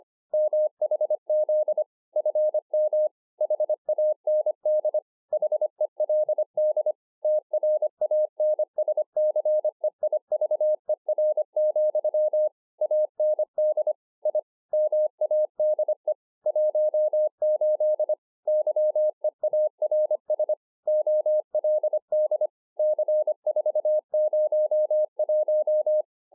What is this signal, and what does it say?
Morse code, heard as a keyed tone, about 25 wpm, 610 Hz. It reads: MHZ FM HAND HELD TRANSCEIVER, AND I MADE 18 YEARS OLD C401